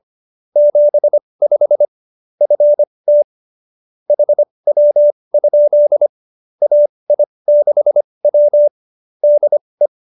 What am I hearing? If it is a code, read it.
Morse code, heard as a keyed tone, about 25 wpm, spaced as for 20 wpm, 590 Hz. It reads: 75 FT HW? AI6W DE
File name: qso207b 20